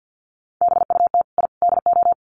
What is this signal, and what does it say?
Morse code, heard as a keyed tone, about 48 wpm, 700 Hz. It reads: BUT I DO
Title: Sets of 3 Words 50wpm